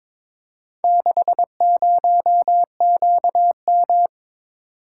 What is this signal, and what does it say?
Morse code, heard as a keyed tone, about 22 wpm, 700 Hz. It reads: 60QM
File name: practice / 22wpm / Four Letter-Number Combinations - 22wpm